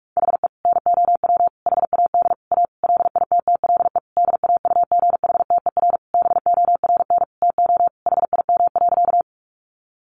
Morse code, heard as a keyed tone, about 44 wpm, 700 Hz. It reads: HE NOW HAD A LITTLE DAUGHTER BORN TO HIM.
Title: Sentences from Top 1500 Words 45wpm